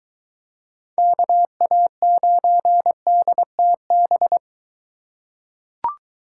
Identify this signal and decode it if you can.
Morse code, heard as a keyed tone, about 22 wpm, 700 Hz. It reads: KA9DTB